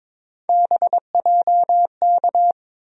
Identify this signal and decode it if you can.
Morse code, heard as a keyed tone, about 22 wpm, 700 Hz. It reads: BJK
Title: USA Call Sign Postfixes - 3 Letters 22wpm